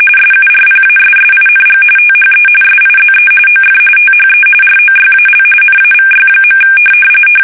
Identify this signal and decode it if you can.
RTTY, about 45.45 baud. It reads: SURPASSING THE SIZE OF THE ROYAL NAVY.
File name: fsk850